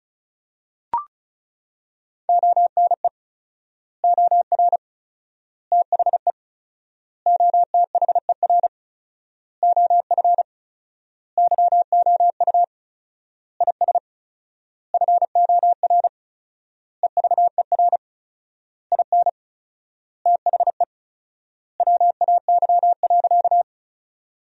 Morse code, heard as a keyed tone, about 34 wpm, 700 Hz. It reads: ONE OR THE OTHER OF YOU IS FOR EVER IN THE WAY.
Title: 4x Word-Spacing - Sentences from Top 400 Words - 35wpm